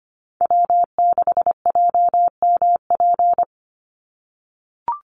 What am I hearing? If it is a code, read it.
Morse code, heard as a keyed tone, about 25 wpm, 700 Hz. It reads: W6JMP
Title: NA SOTA Chaser Callsigns 25wpm